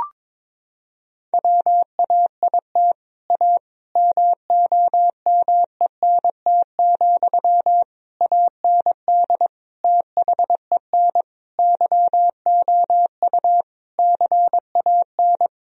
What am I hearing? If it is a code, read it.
Morse code, heard as a keyed tone, about 22 wpm, 700 Hz. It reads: WAIT A MOMENT, AND THEN YOU CAN
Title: Sentences from Top 1000 Words - Review Part 4 - 22wpm